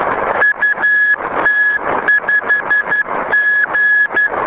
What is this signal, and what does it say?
Morse code, heard as a keyed tone, about 12 wpm, 1700 Hz, with noise. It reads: UT5G
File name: UT5G B